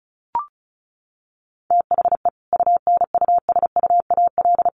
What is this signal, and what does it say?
Morse code, heard as a keyed tone, about 34 wpm, 700 Hz. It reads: THE UNUSUAL
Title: Sentences from Top 1400 Words 35wpm